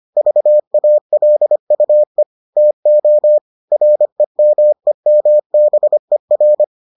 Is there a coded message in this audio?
Yes: Morse code from 0.2 to 6.6 s, heard as a keyed tone, about 25 wpm, 600 Hz. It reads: VALUE TO REMEMBER